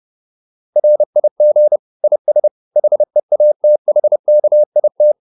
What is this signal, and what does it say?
Morse code, heard as a keyed tone, about 30 wpm, 600 Hz. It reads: RIG IS HEATHKIT
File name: QSO310 30